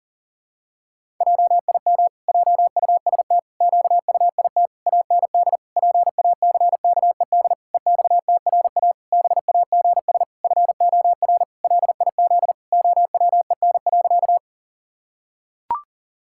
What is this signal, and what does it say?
Morse code, heard as a keyed tone, about 40 wpm, 700 Hz. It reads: JIM JUST QUIT AND PACKED EXTRA BAGS FOR LIZ OWEN.
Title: Pangram Challenge v2 - 40wpm